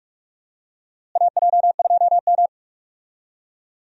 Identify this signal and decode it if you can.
Morse code, heard as a keyed tone, about 46 wpm, 700 Hz. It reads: AJ2M
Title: USA Call Signs - 4 Characters - Repeated 3x - 45wpm